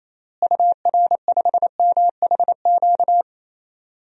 Morse code, heard as a keyed tone, about 28 wpm, 700 Hz. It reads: UR5MHQ
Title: DXCC-100 Call Signs - 6 Characters - Repeated 3x - 28wpm